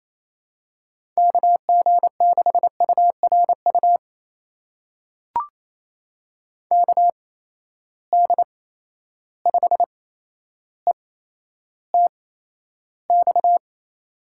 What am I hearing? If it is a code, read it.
Morse code, heard as a keyed tone, about 28 wpm, spaced as for 12 wpm, 700 Hz. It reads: KG6URU KD5ETX
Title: USA Call Signs - 6 Characters - 8x Character Spacing - 28wpm